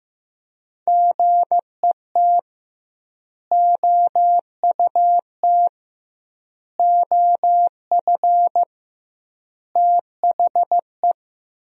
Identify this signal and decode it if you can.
Morse code, heard as a keyed tone, about 16 wpm, 700 Hz. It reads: GET OUT OF THE
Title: 2x Word-Spacing - Sentences from Top 200 Words - 15wpm